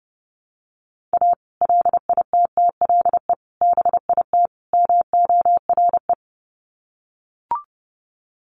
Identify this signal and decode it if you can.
Morse code, heard as a keyed tone, about 30 wpm, 700 Hz. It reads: A LITTLE BIT MORE
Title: Sets of 4 Words 30wpm